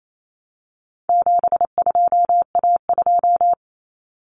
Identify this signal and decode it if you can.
Morse code, heard as a keyed tone, about 28 wpm, 700 Hz. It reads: 72A2
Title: Common Cut Numbers - 4-Digit 28wpm